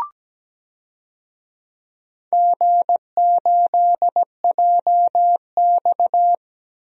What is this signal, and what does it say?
Morse code, heard as a keyed tone, about 17 wpm, 700 Hz. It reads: G8JX
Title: DXCC-100 Call Signs - 4 Characters - Repeated 2x - 17wpm